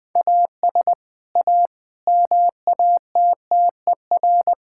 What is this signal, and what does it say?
Morse code, heard as a keyed tone, about 21 wpm, 700 Hz. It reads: AS A MATTER
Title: Sentences with 4-Word Phrases - 20wpm